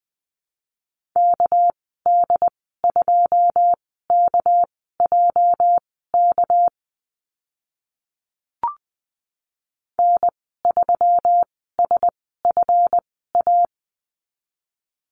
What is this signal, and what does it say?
Morse code, heard as a keyed tone, about 20 wpm, spaced as for 14 wpm, 700 Hz. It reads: KD2KJK N3SFA
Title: 245 Lesson 39g - Letter z - Call Signs 20wpm